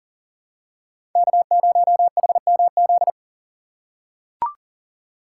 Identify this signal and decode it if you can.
Morse code, heard as a keyed tone, about 40 wpm, 700 Hz. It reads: K0HMZ
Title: K1USN SST QSO Practice - 40wpm